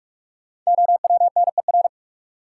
Morse code, heard as a keyed tone, about 44 wpm, 700 Hz. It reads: OWNER
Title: Sentences from Top 1600 Words - New Word Drill Alt - 45wpm